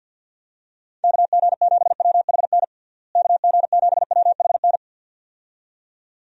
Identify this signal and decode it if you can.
Morse code, heard as a keyed tone, about 51 wpm, 700 Hz. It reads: KG7WHN KG7WHN
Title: USA Call Signs - 6 Characters - Repeated 2x - 50wpm